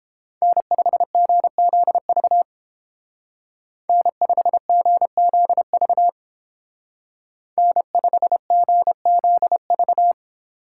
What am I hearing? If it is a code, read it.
Morse code, heard as a keyed tone, about 29 wpm, 700 Hz. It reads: N5GZV N5GZV N5GZV